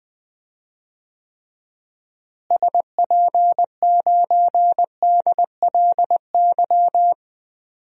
Morse code, heard as a keyed tone, about 20 wpm, 700 Hz. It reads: SP9DLY
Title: DXCC-100 Call Signs - 6 Characters - Repeated 3x - 20wpm